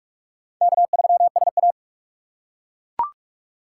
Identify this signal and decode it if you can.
Morse code, heard as a keyed tone, about 39 wpm, 700 Hz. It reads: K3SA